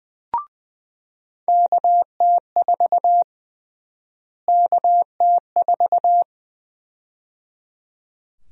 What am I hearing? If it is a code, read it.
Morse code, heard as a keyed tone, about 20 wpm, 700 Hz. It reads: KT4 KT4
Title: USA Call Sign Prefixes - Repeated 2x - 20wpm